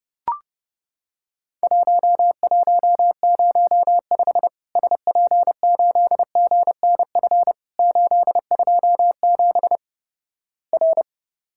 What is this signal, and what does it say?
Morse code, heard as a keyed tone, about 30 wpm, 700 Hz. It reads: 1105 SP8GNF 827 R
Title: Worked All Europe DX Contest - QTC Exchange - 30wpm